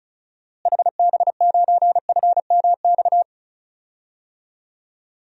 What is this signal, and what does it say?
Morse code, heard as a keyed tone, about 36 wpm, 700 Hz. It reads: HB9FMX
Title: DXCC-100 Call Signs - 6 Characters - Repeated 2x - 35wpm